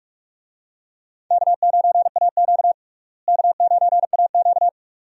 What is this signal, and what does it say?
Morse code, heard as a keyed tone, about 42 wpm, 700 Hz. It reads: K9AQ K9AQ
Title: USA Call Signs - 4 Characters - Repeated 2x - 45wpm